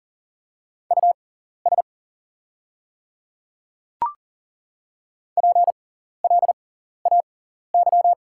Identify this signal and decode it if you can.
Morse code, heard as a keyed tone, about 40 wpm, spaced as for 14 wpm, 700 Hz. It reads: US PLAY
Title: 6x Character Spacing - Top 200 Words - 40wpm